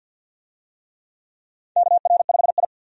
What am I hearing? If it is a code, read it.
Morse code, heard as a keyed tone, about 47 wpm, 700 Hz. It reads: KR5I
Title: USA Call Signs - 4 Characters - 50wpm